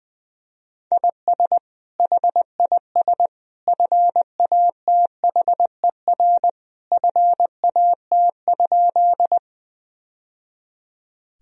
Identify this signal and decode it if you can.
Morse code, heard as a keyed tone, about 20 wpm, 700 Hz. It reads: IS HIS FATHER FAT?
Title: Sentences from Top 1000 Words - Review Part 4 - 20wpm